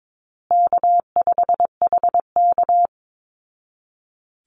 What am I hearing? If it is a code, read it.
Morse code, heard as a keyed tone, about 22 wpm, 700 Hz. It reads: K5HK
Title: Easy Call Signs Repeated 2x - 22wpm